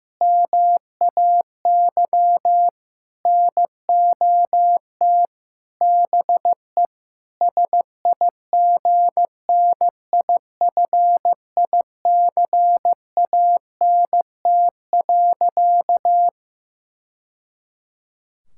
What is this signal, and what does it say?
Morse code, heard as a keyed tone, about 16 wpm, 700 Hz. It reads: MAY NOT BE SIGNIFICANT.
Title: Sentences from Top 1000 Words - Review Part 6 - 15wpm